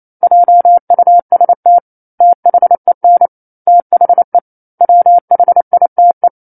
Morse code, heard as a keyed tone, about 28 wpm, 700 Hz. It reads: JUST THEN THE WHITE
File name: 09-Chapter-09 28 WPM